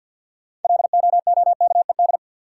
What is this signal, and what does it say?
Morse code, heard as a keyed tone, about 49 wpm, 700 Hz. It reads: LOOKED
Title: Sentences from Top 800 Words - New Word Drill - 50wpm